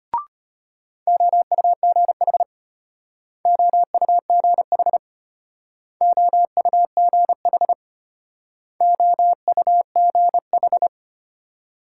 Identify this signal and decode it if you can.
Morse code, heard as a keyed tone, about 34 wpm, 700 Hz. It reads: OUGH OUGH OUGH OUGH